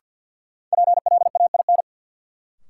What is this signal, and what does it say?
Morse code, heard as a keyed tone, about 53 wpm, 700 Hz. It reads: PLAIN
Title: Sentences from Top 400 Words - New Word Drill - 50wpm